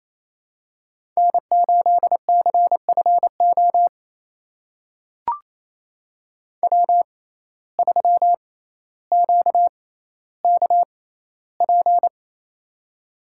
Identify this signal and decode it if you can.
Morse code, heard as a keyed tone, about 28 wpm, spaced as for 14 wpm, 700 Hz. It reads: N8CFO W3QKP
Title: USA Call Signs - 5 Characters - 6x Character Spacing - 28wpm